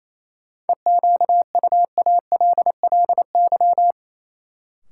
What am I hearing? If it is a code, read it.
Morse code, heard as a keyed tone, about 28 wpm, 700 Hz. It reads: EQUALLY
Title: Sentences from Top 1400 Words - New Word Drill - 28wpm